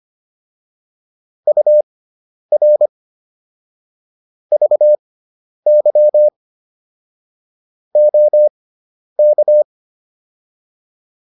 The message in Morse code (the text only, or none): UR VY OK